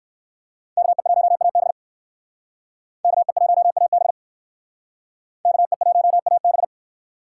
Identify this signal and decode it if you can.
Morse code, heard as a keyed tone, about 56 wpm, 700 Hz. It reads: KE1AB KE1AB KE1AB